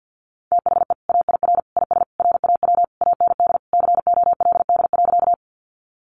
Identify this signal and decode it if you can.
Morse code, heard as a keyed tone, about 50 wpm, 700 Hz. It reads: THE AIR IS RAW AND COLD.
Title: Sentences from Top 1500 Words 50wpm